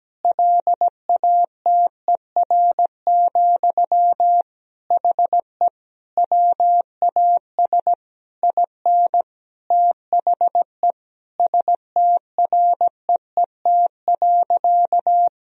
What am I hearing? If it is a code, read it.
Morse code, heard as a keyed tone, about 17 wpm, 700 Hz. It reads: LATER, HE WAS IN THE STREET.
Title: Sentences from Top 1000 Words - Review Part 4 - 17wpm